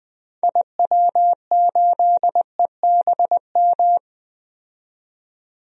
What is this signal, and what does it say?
Morse code, heard as a keyed tone, about 20 wpm, 700 Hz. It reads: IW8EBM
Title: DXCC-100 Call Signs - 6 Characters - 20wpm